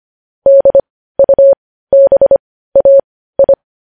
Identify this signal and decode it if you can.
Morse code, heard as a keyed tone, about 25 wpm, spaced as for 15 wpm, 550 Hz. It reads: DUBAI
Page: news-headlines-in-morse-code-at-15-wpm